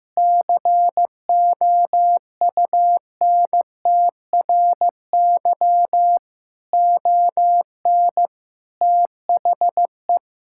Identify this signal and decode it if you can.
Morse code, heard as a keyed tone, about 15 wpm, 700 Hz. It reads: COUNTRY ON THE
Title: Sentences from Top 1000 Words - Review Part 6 - 15wpm